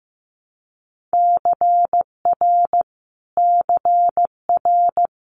Morse code, heard as a keyed tone, about 15 wpm, 700 Hz. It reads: CR CR
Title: Multi-letter Phonemes Repeated 3x 15wpm